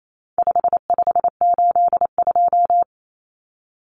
Morse code, heard as a keyed tone, about 28 wpm, 700 Hz. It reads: 5582
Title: Numbers - 4-Digit 28wpm